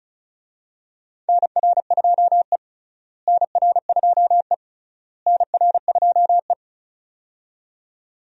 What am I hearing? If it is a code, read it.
Morse code, heard as a keyed tone, about 36 wpm, 700 Hz. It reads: NR2E NR2E NR2E
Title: USA Call Signs - 4 Characters - Repeated 3x - 35wpm